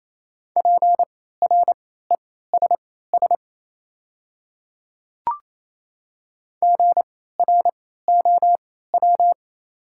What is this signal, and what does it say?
Morse code, heard as a keyed tone, about 28 wpm, spaced as for 16 wpm, 700 Hz. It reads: PRESS GROW
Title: 3x Character Spacing - Top 300 Words - 28wpm